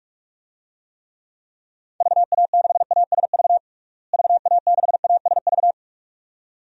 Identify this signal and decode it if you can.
Morse code, heard as a keyed tone, about 46 wpm, 700 Hz. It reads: VA6ASV VA6ASV
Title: DXCC-100 Call Signs - 6 Characters - Repeated 2x - 45wpm